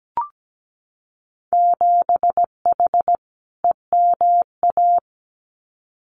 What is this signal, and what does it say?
Morse code, heard as a keyed tone, about 17 wpm, 700 Hz. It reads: 7H EMA
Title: Winter Field Day Exchanges Encore 17wpm